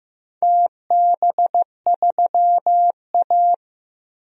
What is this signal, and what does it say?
Morse code, heard as a keyed tone, about 15 wpm, 700 Hz. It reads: TB3A
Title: DXCC-100 Call Signs - 4 Characters - Repeated 2x - 15wpm